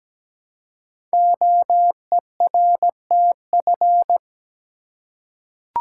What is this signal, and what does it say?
Morse code, heard as a keyed tone, about 17 wpm, 700 Hz. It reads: OERTF
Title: Code Groups - 5 Letters - 17wpm